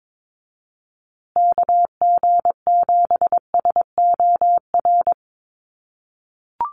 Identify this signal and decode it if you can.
Morse code, heard as a keyed tone, about 22 wpm, 700 Hz. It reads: KG7SOR
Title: USA Call Signs Repeated 2x 22wpm